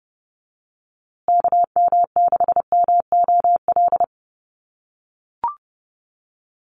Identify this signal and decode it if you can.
Morse code, heard as a keyed tone, about 30 wpm, 700 Hz. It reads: KM6MOL
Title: USA Call Signs Repeated 2x 30wpm